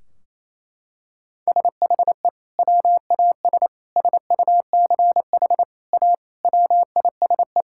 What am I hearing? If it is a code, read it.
Morse code, heard as a keyed tone, about 28 wpm, 700 Hz. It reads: SHE WAS SUCH A WISE
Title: Sentences from Top 1600 Words 28wpm